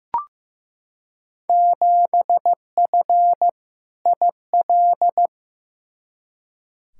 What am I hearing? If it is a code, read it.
Morse code, heard as a keyed tone, about 15 wpm, 700 Hz. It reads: 7F IL
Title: Field Day Exchanges Encore - 15wpm